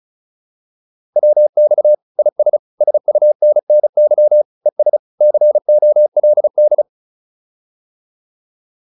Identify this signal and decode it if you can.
Morse code, heard as a keyed tone, about 34 wpm, 600 Hz. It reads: WX IS SUNNY ES COLD